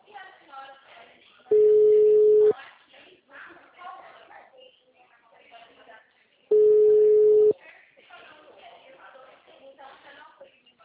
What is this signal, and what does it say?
Ringback tone (Europe, 425 Hz), with speech.